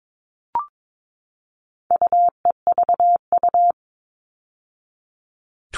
Morse code, heard as a keyed tone, about 22 wpm, 700 Hz. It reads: UEVU
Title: All Cut Numbers - 4-Digit 22wpm